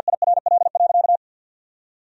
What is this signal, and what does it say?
Morse code, heard as a keyed tone, about 53 wpm, 700 Hz. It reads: IRL.
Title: 2x Word-Spacing - Sentences from Top 400 Words - 50wpm